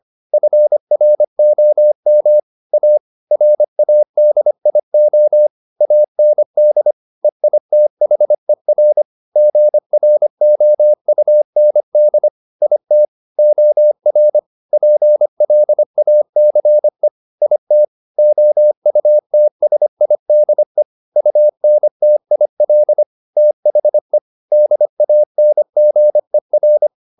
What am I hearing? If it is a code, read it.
Morse code, heard as a keyed tone, about 25 wpm, 600 Hz. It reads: FROM A RADIO AND EITHER GROUND IT OR PLACE IT OUTSIDE UNTIL THE DANGER